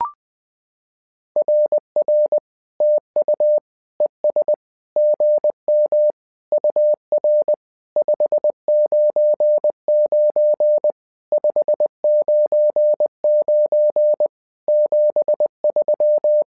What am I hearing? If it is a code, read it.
Morse code, heard as a keyed tone, about 20 wpm, 600 Hz. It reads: RR TU ES GM UR 599 599 73
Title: SOTA QSO Practice 20wpm